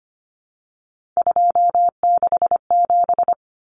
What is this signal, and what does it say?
Morse code, heard as a keyed tone, about 25 wpm, 700 Hz. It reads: 267
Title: Common Cut Numbers - 3-Digit 25wpm